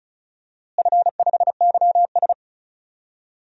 Morse code, heard as a keyed tone, about 36 wpm, 700 Hz. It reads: F5YS
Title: DXCC-100 Call Signs - 4 Characters - Repeated 2x - 35wpm